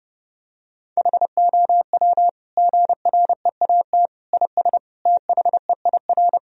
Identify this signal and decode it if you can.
Morse code, heard as a keyed tone, about 30 wpm, 700 Hz. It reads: HOW GREAT IS THEIR
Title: Sentences from Top 1000 Words - Review Part 4 - 30wpm